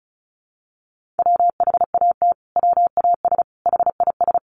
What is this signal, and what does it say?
Morse code, heard as a keyed tone, about 34 wpm, 700 Hz. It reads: WHAT WAS HIS
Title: Sentences from Top 1400 Words 35wpm